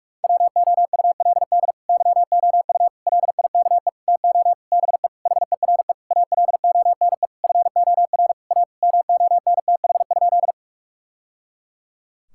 Morse code, heard as a keyed tone, about 41 wpm, 700 Hz. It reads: WOULD YOU LIKE TO BE HERE ALONE FOR A MONTH?
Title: Sentences from Top 1000 Words - Review Part 5 - 45wpm